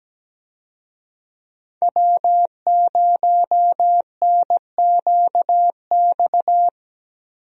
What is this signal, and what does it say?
Morse code, heard as a keyed tone, about 17 wpm, 700 Hz. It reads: W0NQX
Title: USA Call Signs - 5 Characters - Repeated 2x - 17wpm